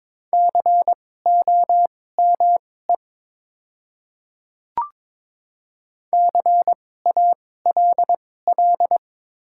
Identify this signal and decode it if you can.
Morse code, heard as a keyed tone, about 22 wpm, spaced as for 16 wpm, 700 Hz. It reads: COME CALL
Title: 2x Character Spacing - Top 100 Words - 22wpm